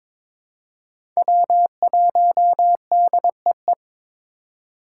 Morse code, heard as a keyed tone, about 22 wpm, 700 Hz. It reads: W1DEE
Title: USA Call Signs - 2x Character Spacing - 22wpm